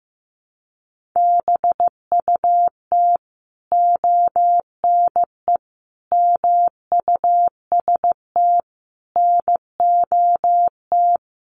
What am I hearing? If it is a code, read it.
Morse code, heard as a keyed tone, about 15 wpm, 700 Hz. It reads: BUT ONE MUST NOT
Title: Sentences from Top 1500 Words 15wpm